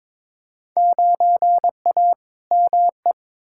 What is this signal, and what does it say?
Morse code, heard as a keyed tone, about 22 wpm, 700 Hz. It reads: 9A ME
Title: Field Day Exchanges Encore - 22wpm